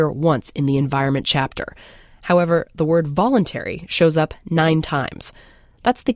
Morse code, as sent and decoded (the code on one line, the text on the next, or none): none